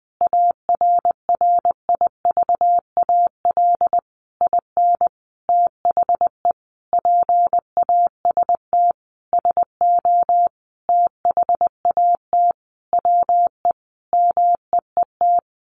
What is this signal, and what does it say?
Morse code, heard as a keyed tone, about 20 wpm, 700 Hz. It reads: ARRIVAL IN THE PAST SO THAT WE MEET
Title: A Sound Of Thunder 20wpm